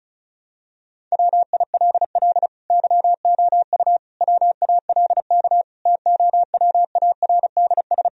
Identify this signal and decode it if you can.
Morse code, heard as a keyed tone, about 34 wpm, 700 Hz. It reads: WILL YOU WALK TOWARDS
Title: Sentences from Top 1000 Words - Review Part 2 - 35wpm